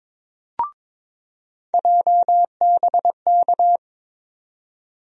Morse code, heard as a keyed tone, about 21 wpm, 700 Hz. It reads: JBK
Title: USA Call Sign Postfixes - 3 Letters 22wpm